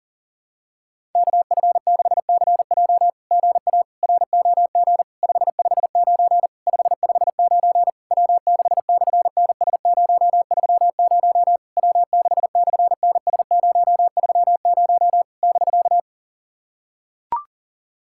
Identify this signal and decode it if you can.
Morse code, heard as a keyed tone, about 38 wpm, 700 Hz. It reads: KF6CJ GA ROG 559 559 W6/NS030 W6/NS030 <BK>
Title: SOTA QSO Practice 40wpm